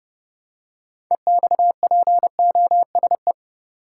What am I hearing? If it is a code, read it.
Morse code, heard as a keyed tone, about 30 wpm, 700 Hz. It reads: EXPOSE
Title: Six Letter Words - 30wpm